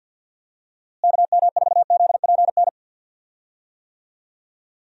Morse code, heard as a keyed tone, about 52 wpm, 700 Hz. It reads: KM4ZPN
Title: USA Call Signs - 6 Characters - Repeated 2x - 50wpm